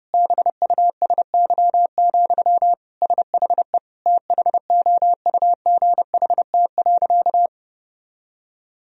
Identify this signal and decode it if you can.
Morse code, heard as a keyed tone, about 32 wpm, 700 Hz. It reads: BUSY, SHE THOUGHT.
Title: Sentences from Top 1000 Words - Review Part 3 - 30wpm